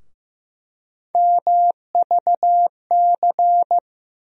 Morse code, heard as a keyed tone, about 15 wpm, 700 Hz. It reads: MVC
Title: USA Call Sign Postfixes - 3 Letters 15wpm